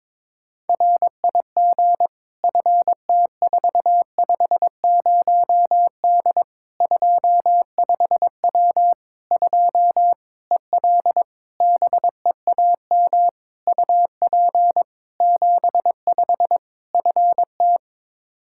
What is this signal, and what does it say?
Morse code, heard as a keyed tone, about 22 wpm, 700 Hz. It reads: RIG FT450D 25W 2 EL BEAM UP 75 FT
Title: Practice QSO - Exchange 2 - Info - 22wpm